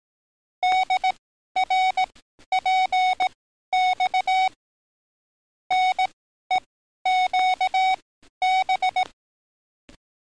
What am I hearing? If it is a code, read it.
Morse code, heard as a keyed tone, about 18 wpm, spaced as for 12 wpm, 740 Hz. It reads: DRPX NEQB
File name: training / NAVPERS 11049to11060 / NAVPERS 11053 RA RB